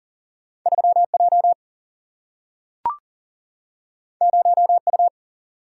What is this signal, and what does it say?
Morse code, heard as a keyed tone, about 40 wpm, 700 Hz. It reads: 3J 0U
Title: Single-Letter Single-Number 40wpm